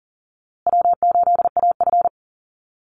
Morse code, heard as a keyed tone, about 40 wpm, 700 Hz. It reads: W8AF
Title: USA Call Signs Repeated 2x 40wpm